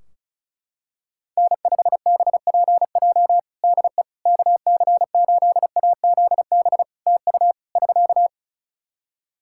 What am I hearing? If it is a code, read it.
Morse code, heard as a keyed tone, about 36 wpm, 700 Hz. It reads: N5BPJ DE KC8AZB TU <SK>
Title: K1USN SST QSO Practice 35wpm